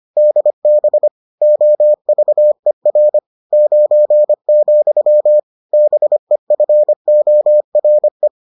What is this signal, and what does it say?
Morse code, heard as a keyed tone, about 25 wpm, 600 Hz. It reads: DB OVER 9, BEFORE